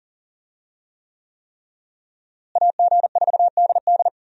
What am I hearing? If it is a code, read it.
Morse code, heard as a keyed tone, about 40 wpm, 700 Hz. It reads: AG4DD
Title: USA Call Signs - 5 Characters - Repeated 3x - 40wpm